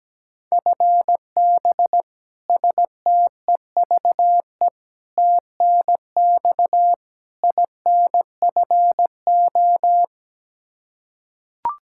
Morse code, heard as a keyed tone, about 17 wpm, 700 Hz. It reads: FB STEVE TNX INFO
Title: Practice QSO - Exchange 3 - More Info - 17wpm